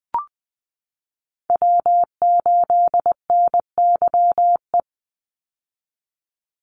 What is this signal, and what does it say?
Morse code, heard as a keyed tone, about 19 wpm, 700 Hz. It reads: W8NYE
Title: USA Call Signs 20wpm